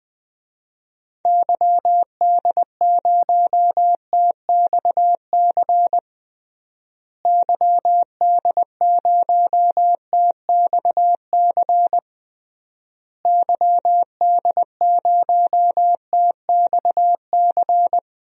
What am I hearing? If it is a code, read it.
Morse code, heard as a keyed tone, about 20 wpm, 700 Hz. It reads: YD0TXC YD0TXC YD0TXC
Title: DXCC-100 Call Signs - 6 Characters - Repeated 3x - 20wpm